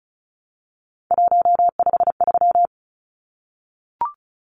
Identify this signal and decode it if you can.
Morse code, heard as a keyed tone, about 34 wpm, 700 Hz. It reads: 153
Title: Numbers - 3-Digit 35wpm